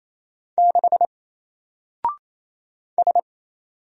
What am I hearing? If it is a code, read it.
Morse code, heard as a keyed tone, about 27 wpm, 700 Hz. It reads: 6 S
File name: Single-Letter-Number-28wpm